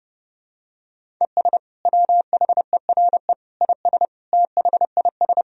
Morse code, heard as a keyed tone, about 32 wpm, 700 Hz. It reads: ES WHERE IS THIS
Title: Sentences from Top 200 Words and Abbreviations 30wpm